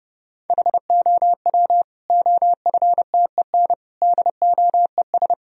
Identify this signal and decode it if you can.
Morse code, heard as a keyed tone, about 32 wpm, 700 Hz. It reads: HOW OFTEN DOES
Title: Sentences from Top 1000 Words - Review Part 3 - 30wpm